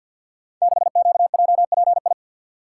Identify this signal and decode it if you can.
Morse code, heard as a keyed tone, about 46 wpm, 700 Hz. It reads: 6QJPI
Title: Code Groups - 5 Letter:Numbers - 50wpm